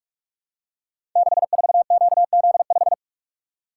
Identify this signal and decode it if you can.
Morse code, heard as a keyed tone, about 44 wpm, 700 Hz. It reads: 64QZ5
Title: Code Groups - 5 Letter:Numbers - 45wpm